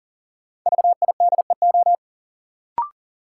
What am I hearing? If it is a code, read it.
Morse code, heard as a keyed tone, about 40 wpm, 700 Hz. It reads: VIDEO
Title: Sentences from Top 1000 Words - Review Part 4 - New Word Drill - 40wpm